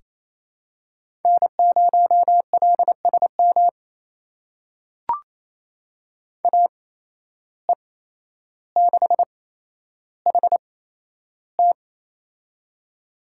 Morse code, heard as a keyed tone, about 28 wpm, spaced as for 12 wpm, 700 Hz. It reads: N0LSM AE6HT
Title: USA Call Signs - 5 Characters - 8x Character Spacing - 28wpm